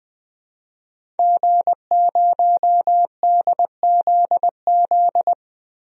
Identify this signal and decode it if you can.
Morse code, heard as a keyed tone, about 20 wpm, 700 Hz. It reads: G0DZZ
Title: DXCC-100 Call Signs - 5 Characters - Repeated 3x - 20wpm